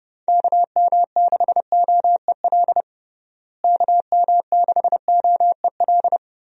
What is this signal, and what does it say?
Morse code, heard as a keyed tone, about 30 wpm, 700 Hz. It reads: KM6OEL KM6OEL
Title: USA Call Signs - 6 Characters - Repeated 3x - 30wpm